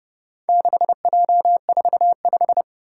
Morse code, heard as a keyed tone, about 29 wpm, 700 Hz. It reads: 6J45